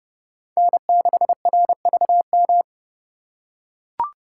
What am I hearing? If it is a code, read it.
Morse code, heard as a keyed tone, about 29 wpm, 700 Hz. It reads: N6RVM
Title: USA Call Signs - 5 Characters - 2x Character Spacing - 30wpm